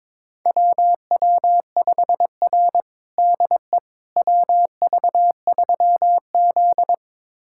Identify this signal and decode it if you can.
Morse code, heard as a keyed tone, about 22 wpm, 700 Hz. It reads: WW5R DE WV3Z
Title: USA Callsign DE Callsign - 4 Characters - 22wpm